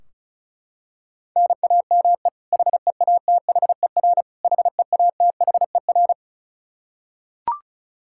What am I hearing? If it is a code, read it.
Morse code, heard as a keyed tone, about 34 wpm, 700 Hz. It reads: NAME HEATHER HEATHER
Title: K1USN SST QSO Practice 35wpm